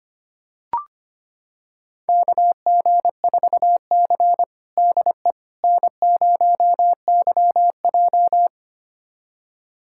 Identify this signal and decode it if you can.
Morse code, heard as a keyed tone, about 25 wpm, 700 Hz. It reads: KG4C DE N0YJ
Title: USA Callsign DE Callsign - 4 Characters - 25wpm